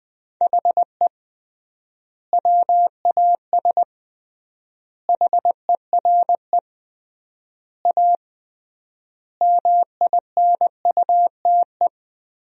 Morse code, heard as a keyed tone, about 21 wpm, 700 Hz. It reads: HE WAS HERE A MINUTE
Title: 3x Word-Spacing - Sentences from Top 500 Words - 20wpm